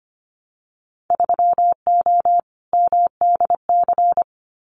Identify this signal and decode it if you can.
Morse code, heard as a keyed tone, about 25 wpm, 690 Hz. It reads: 3O MDC
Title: Winter Field Day Exchanges 25wpm